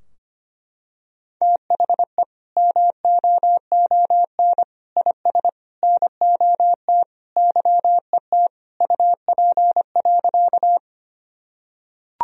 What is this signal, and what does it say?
Morse code, heard as a keyed tone, about 25 wpm, 700 Hz. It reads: THE MOON IS NOT YET UP.
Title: Sentences from Top 1000 Words - Review Part 3 - 25wpm